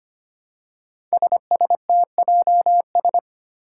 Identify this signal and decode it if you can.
Morse code, heard as a keyed tone, about 25 wpm, 700 Hz. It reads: SSTJS
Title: Five Challenging Letter-Number Combinations 25wpm